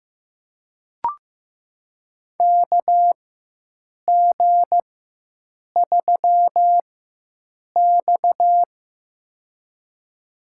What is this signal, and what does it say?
Morse code, heard as a keyed tone, about 15 wpm, spaced as for 7 wpm, 700 Hz. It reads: KG3X
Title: USA Call Signs - 4x Character Spacing - 15wpm